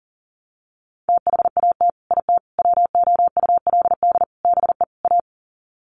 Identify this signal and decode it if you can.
Morse code, heard as a keyed tone, about 40 wpm, 700 Hz. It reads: THAT IT WOULD BE A
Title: Sets of 5 Words Encore2 40wpm